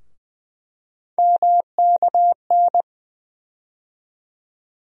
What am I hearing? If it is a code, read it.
Morse code, heard as a keyed tone, about 20 wpm, 700 Hz. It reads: MKN
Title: USA Call Sign Postfixes - 3 Letters 20wpm